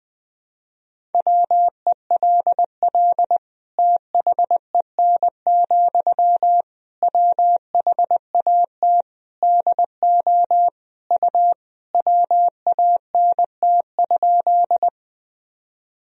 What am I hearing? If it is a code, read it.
Morse code, heard as a keyed tone, about 20 wpm, 700 Hz. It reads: WELL THEN, WHAT DO U WANT?
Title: Sentences from Top 200 Words and Abbreviations 20wpm